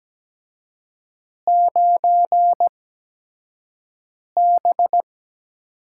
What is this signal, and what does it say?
Morse code, heard as a keyed tone, about 17 wpm, spaced as for 5 wpm, 700 Hz. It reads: 9B
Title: USA Call Signs - 4 Characters - 8x Character Spacing - 17wpm